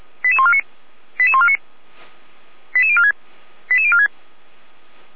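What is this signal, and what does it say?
Five-tone selective call (ZVEI), heard as quick runs of five tones, tones of about 70 ms. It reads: 88149 88149 88046 88046
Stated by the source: Five-tone ZVEI selective call 88402, received by SDR, fire brigade Orlamünde (feuerwehr-orlamuende).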